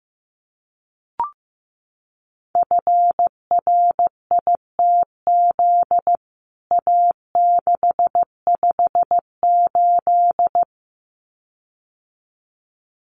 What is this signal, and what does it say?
Morse code, heard as a keyed tone, about 15 wpm, 700 Hz. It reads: FRITZ A658
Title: CWT Exchange 15wpm